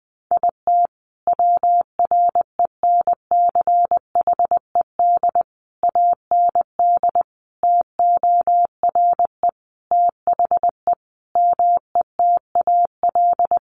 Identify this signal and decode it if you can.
Morse code, heard as a keyed tone, about 20 wpm, 700 Hz. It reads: IT WRENCHED AND TORE THE METAL
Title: A Sound Of Thunder 20wpm